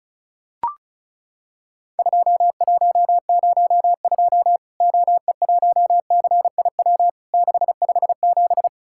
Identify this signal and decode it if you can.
Morse code, heard as a keyed tone, about 34 wpm, 700 Hz. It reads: 2102 OE1CIW 657
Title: Worked All Europe DX Contest - QTC Exchange - 35wpm